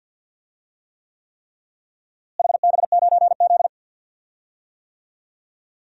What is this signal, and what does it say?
Morse code, heard as a keyed tone, about 47 wpm, 700 Hz. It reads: HB9Z
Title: DXCC-100 Call Signs - 4 Characters - Repeated 3x - 50wpm